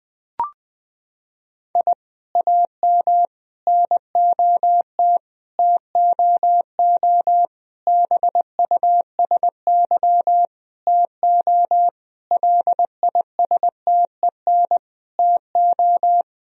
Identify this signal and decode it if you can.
Morse code, heard as a keyed tone, about 21 wpm, 700 Hz. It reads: I AM NOT TOO BUSY TO LISTEN TO
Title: Sentences from Top 1000 Words - Review Part 3 - 20wpm